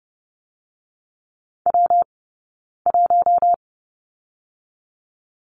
Morse code, heard as a keyed tone, about 30 wpm, 700 Hz. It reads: W 1
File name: Lesson 20d - Number 3 - Two Characters 30wpm